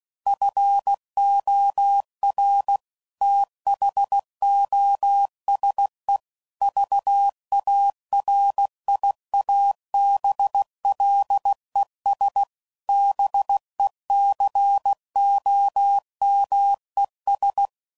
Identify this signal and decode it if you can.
Morse code, heard as a keyed tone, about 16 wpm, 800 Hz. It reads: FOR THOSE VARIABLES BECOMES
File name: CW Code Groups @15WPM / Technical Subjects / AND Gate